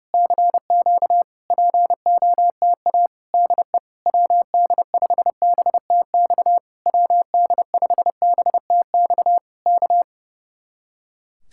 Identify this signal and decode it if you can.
Morse code, heard as a keyed tone, about 28 wpm, 700 Hz. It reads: CQ POTA DE WD5BTX WD5BTX K
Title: Speed-Racing Common QSO Phrases - 30wpm